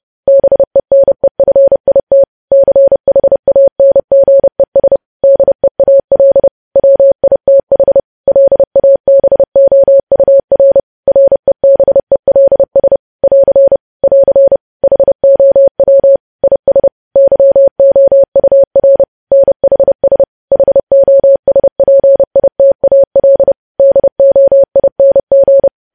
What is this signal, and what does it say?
Morse code, heard as a keyed tone, about 30 wpm, 550 Hz. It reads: BENEFIT CHANGES DEAL WITH LABOUR REBELS <AR> <AR> HOW IS YOUR NHS HOSPITAL DOING
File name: podcast / News headlines-1x-30x30